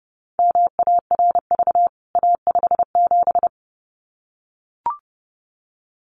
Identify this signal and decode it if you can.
Morse code, heard as a keyed tone, about 30 wpm, 700 Hz. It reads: MARV A57
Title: CWT Exchange 30wpm